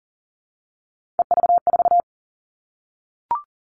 Morse code, heard as a keyed tone, about 40 wpm, 700 Hz. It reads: EV4
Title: All Cut Numbers - 3-Digit 40wpm